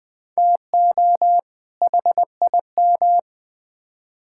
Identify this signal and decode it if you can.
Morse code, heard as a keyed tone, about 20 wpm, 700 Hz. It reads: TO HIM
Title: Build A Phrase - 2-Word Phrases - 20wpm